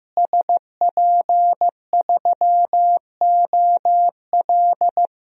Morse code, heard as a keyed tone, about 15 wpm, 700 Hz. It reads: SP3OL
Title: DXCC-100 Call Signs - 5 Characters - Repeated 2x - 15wpm